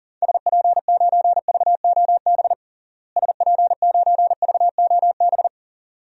Morse code, heard as a keyed tone, about 40 wpm, 700 Hz. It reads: SP9VOB SP9VOB
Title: DXCC-100 Call Signs - 6 Characters - Repeated 2x - 40wpm